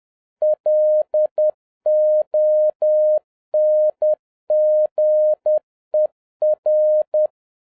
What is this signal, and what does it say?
Morse code, heard as a keyed tone, about 10 wpm, 610 Hz. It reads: LONGER